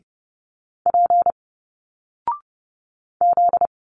Morse code, heard as a keyed tone, about 30 wpm, 700 Hz. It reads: P Z
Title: Single Letters 30wpm